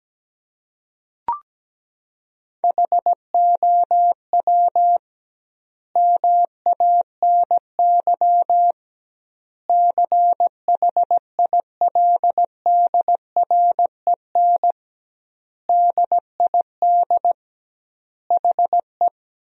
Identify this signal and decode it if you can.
Morse code, heard as a keyed tone, about 17 wpm, 700 Hz. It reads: HOW MANY CHILDREN DID HE
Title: 2x Word-Spacing - Sentences from Top 400 Words - 17wpm